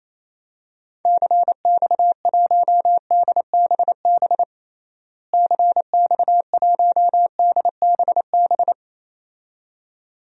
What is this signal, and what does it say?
Morse code, heard as a keyed tone, about 28 wpm, 700 Hz. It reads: CX1DBB CX1DBB
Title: DXCC-100 Call Signs - 6 Characters - Repeated 2x - 28wpm